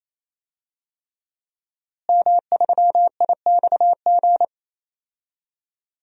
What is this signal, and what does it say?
Morse code, heard as a keyed tone, about 27 wpm, 700 Hz. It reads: M3IXG
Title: DXCC-100 Call Signs - 5 Characters - Repeated 3x - 28wpm